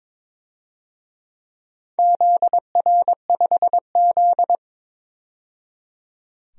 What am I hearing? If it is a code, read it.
Morse code, heard as a keyed tone, about 22 wpm, 700 Hz. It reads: ZR5Z
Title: DXCC-100 Call Signs - 4 Characters - 22wpm